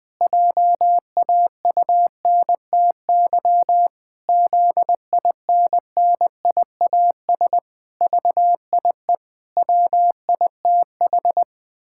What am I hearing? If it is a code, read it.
Morse code, heard as a keyed tone, about 19 wpm, 700 Hz. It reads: JAUNTY ZINNIAS VIE WITH
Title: Pangram Challenge v2 - 20wpm